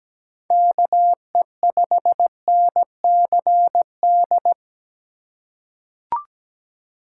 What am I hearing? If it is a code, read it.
Morse code, heard as a keyed tone, about 17 wpm, 700 Hz. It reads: KE5NCD
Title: USA Call Signs - 6 Characters - 4x Character Spacing - 17wpm